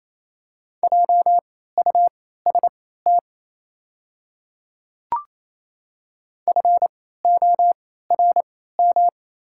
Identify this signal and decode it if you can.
Morse code, heard as a keyed tone, about 28 wpm, spaced as for 16 wpm, 700 Hz. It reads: JUST FORM
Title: 3x Character Spacing - Top 200 Words - 28wpm